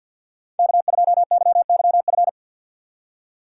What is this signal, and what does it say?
Morse code, heard as a keyed tone, about 52 wpm, 700 Hz. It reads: K2YYF
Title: USA Call Signs - 4x Character Spacing - 50wpm